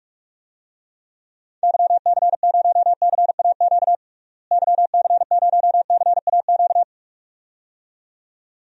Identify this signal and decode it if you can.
Morse code, heard as a keyed tone, about 46 wpm, 700 Hz. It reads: YC0CAQ YC0CAQ
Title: DXCC-100 Call Signs - 6 Characters - Repeated 2x - 45wpm